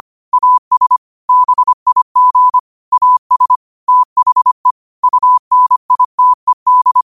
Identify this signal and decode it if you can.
Morse code, heard as a keyed tone, about 25 wpm, 1000 Hz. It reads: AS BIG AS THE UNITED
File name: qotdfiles / 25wpm